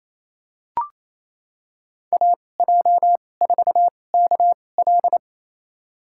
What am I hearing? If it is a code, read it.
Morse code, heard as a keyed tone, about 28 wpm, spaced as for 20 wpm, 700 Hz. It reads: AJ4KL
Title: USA Call Signs - 5 Characters - 2x Character Spacing - 28wpm